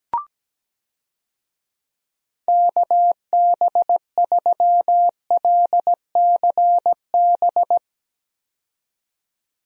Morse code, heard as a keyed tone, about 17 wpm, 700 Hz. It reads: KB3LCB
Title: USA Call Signs - 6 Characters - 17wpm